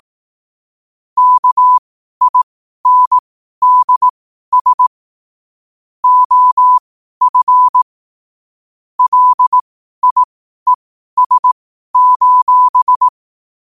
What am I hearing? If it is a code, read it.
Morse code, heard as a keyed tone, about 18 wpm, spaced as for 12 wpm, 1000 Hz. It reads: KINDS OF LIES: